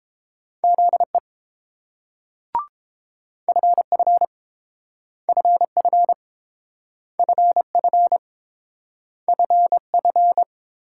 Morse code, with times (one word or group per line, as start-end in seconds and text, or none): 0.6-1.2 ZE
3.5-4.2 FF
5.3-6.1 FF
7.2-8.2 FF
9.3-10.4 FF